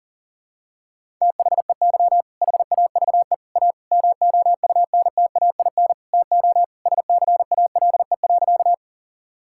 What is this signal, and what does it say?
Morse code, heard as a keyed tone, about 40 wpm, 700 Hz. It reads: THEY HAVE A MOUNTAIN TO SCALE.
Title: Sentences from Top 1000 Words - Review Part 5 - 40wpm